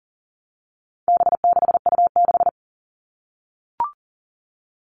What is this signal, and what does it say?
Morse code, heard as a keyed tone, about 40 wpm, 700 Hz. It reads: B6U6
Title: All Cut Numbers - 4-Digit 40wpm